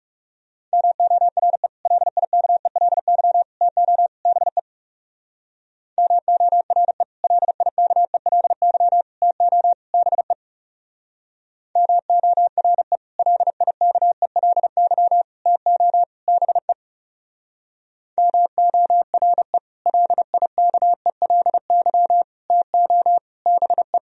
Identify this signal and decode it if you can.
Morse code, heard as a keyed tone, about 37 wpm, 700 Hz. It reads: MORE LIKELY TO BE MORE LIKELY TO BE MORE LIKELY TO BE MORE LIKELY TO BE
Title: Speed-Racing - Sets of 4 Words 30wpm